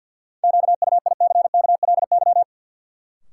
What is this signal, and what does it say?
Morse code, heard as a keyed tone, about 52 wpm, 700 Hz. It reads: QUICKLY
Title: Sentences from Top 700 Words - New Word Drill - 50wpm